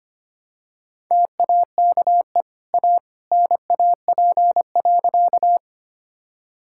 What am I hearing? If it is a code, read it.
Morse code, heard as a keyed tone, about 25 wpm, 700 Hz. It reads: TAKE A NAP.
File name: Easy Sentences with Hints - 25wpm